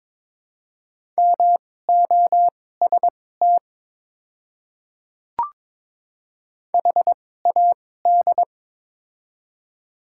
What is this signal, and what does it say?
Morse code, heard as a keyed tone, about 22 wpm, spaced as for 16 wpm, 700 Hz. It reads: MOST HAD
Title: 2x Character Spacing - Top 100 Words - 22wpm